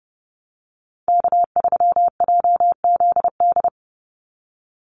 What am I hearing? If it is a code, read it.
Morse code, heard as a keyed tone, about 30 wpm, 700 Hz. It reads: K3JZD
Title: NA SOTA Chaser Callsigns 30wpm